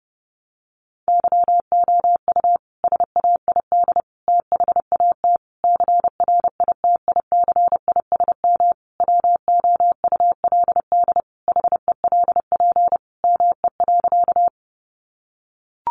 Morse code, heard as a keyed tone, about 28 wpm, 700 Hz. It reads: YOU SAID THAT CRITICISM WOULD HELP ME.
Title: Sentences from Top 1300 Words 30wpm